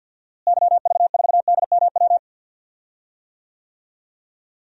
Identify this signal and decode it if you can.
Morse code, heard as a keyed tone, about 53 wpm, 700 Hz. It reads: YV4DMW
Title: DXCC-100 Call Signs - 6 Characters - Repeated 2x - 50wpm